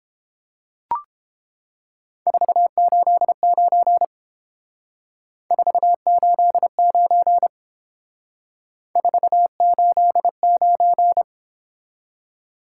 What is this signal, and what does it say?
Morse code, heard as a keyed tone, about 28 wpm, 700 Hz. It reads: 489 489 489